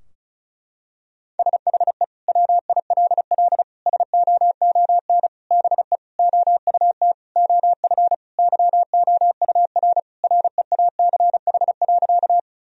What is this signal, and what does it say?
Morse code, heard as a keyed tone, about 36 wpm, 700 Hz. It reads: SHE WILL SOON BE OUT OF YOUR REACH.
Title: Sentences from Top 1000 Words - Review Part 2 - 35wpm